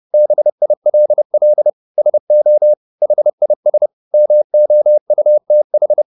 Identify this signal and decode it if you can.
Morse code, heard as a keyed tone, about 30 wpm, 600 Hz. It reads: BILL SO HIS MOUTH